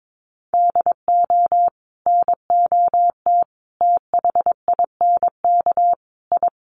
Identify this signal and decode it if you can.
Morse code, heard as a keyed tone, about 22 wpm, 700 Hz. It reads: DO NOT THINK I
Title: Sets of 4 Words 22wpm